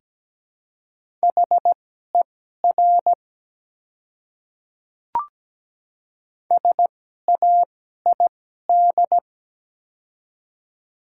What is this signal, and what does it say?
Morse code, heard as a keyed tone, about 17 wpm, spaced as for 12 wpm, 700 Hz. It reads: HER SAID
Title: 2x Character Spacing - Top 100 Words - 17wpm